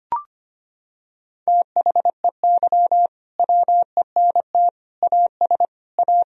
Morse code, heard as a keyed tone, about 25 wpm, 700 Hz. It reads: THEY WENT AS A
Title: Sentences from Top 1000 Words - Review Part 1 - 25wpm